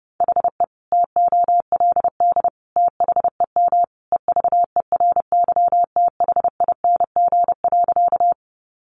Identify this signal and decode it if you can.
Morse code, heard as a keyed tone, about 30 wpm, 700 Hz. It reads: HE TOLD THEM EVERYTHING.
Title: Sentences from Top 700 Words 30wpm